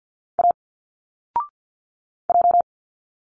Morse code, heard as a keyed tone, about 50 wpm, 700 Hz. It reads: A J
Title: Single Letters 50wpm